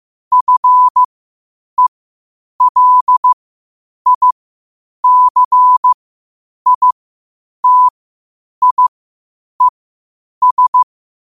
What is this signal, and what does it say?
Morse code, heard as a keyed tone, about 15 wpm, spaced as for 8 wpm, 1000 Hz. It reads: FELICITIES
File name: qotdfiles / 8wpm